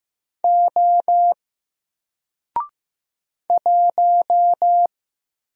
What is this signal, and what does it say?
Morse code, heard as a keyed tone, about 15 wpm, 700 Hz. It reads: O 1
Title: Single Letter-Number Alt 15wpm